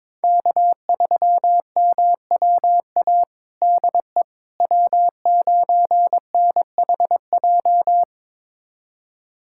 Morse code, heard as a keyed tone, about 22 wpm, 700 Hz. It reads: K3MWA DE W9NHJ
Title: USA Callsign DE Callsign - 5 Characters - 22wpm